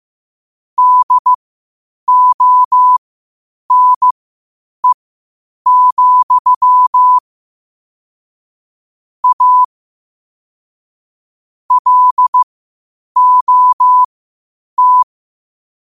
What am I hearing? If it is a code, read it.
Morse code, heard as a keyed tone, about 15 wpm, spaced as for 8 wpm, 1000 Hz. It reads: DONE, A LOT
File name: 8wpm